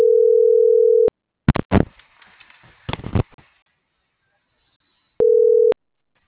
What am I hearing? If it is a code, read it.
Ringback tone (North America), 440 + 480 Hz.